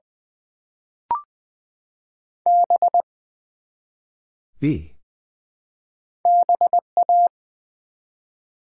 Morse code, heard as a keyed tone, about 20 wpm, 700 Hz. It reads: B BA
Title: Build A Word - 1 Letter At A Time - 3-Letter Words - 20wpm